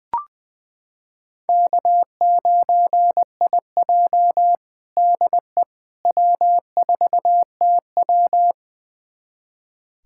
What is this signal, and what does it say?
Morse code, heard as a keyed tone, about 21 wpm, 700 Hz. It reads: K9IJ DE W4TW
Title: USA Callsign DE Callsign - 4 Characters - 20wpm